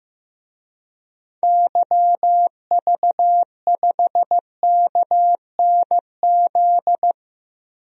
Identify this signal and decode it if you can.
Morse code, heard as a keyed tone, about 15 wpm, 700 Hz. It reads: YV5KNZ
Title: DXCC-100 Call Signs - 6 Characters - Repeated 3x - 15wpm